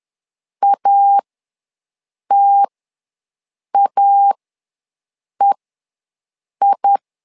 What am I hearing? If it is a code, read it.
Morse code, heard as a keyed tone, about 11 wpm, spaced as for 6 wpm, 800 Hz. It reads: ATAEI
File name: LEZIONI / 2 UHDN / 2 U ws 10wpm 2017